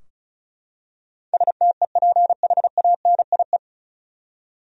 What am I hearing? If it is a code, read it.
Morse code, heard as a keyed tone, about 36 wpm, 700 Hz. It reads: STEPHANIE
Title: USA Common Names - 35wpm